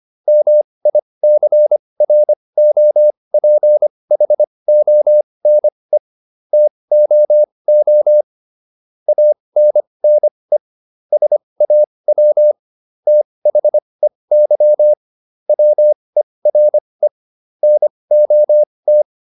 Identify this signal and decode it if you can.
Morse code, heard as a keyed tone, about 25 wpm, spaced as for 20 wpm, 600 Hz. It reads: MICROPHONE TOO ANNE SAW THEY WERE NOT